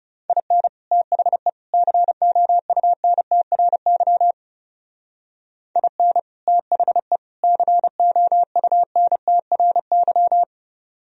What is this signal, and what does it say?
Morse code, heard as a keyed tone, about 36 wpm, 700 Hz. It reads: IN THE COUNTRY IN THE COUNTRY
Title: Speed-Racing - Sets of 3 Words 30wpm